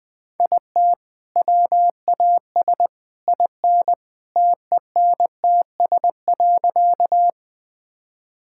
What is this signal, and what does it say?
Morse code, heard as a keyed tone, about 20 wpm, 700 Hz. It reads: IT WAS IN TENTS.
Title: All-Age Appropriate Jokes - 20wpm